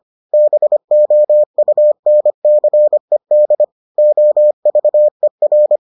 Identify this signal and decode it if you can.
Morse code, heard as a keyed tone, about 25 wpm, 600 Hz. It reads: BOUNCED OVER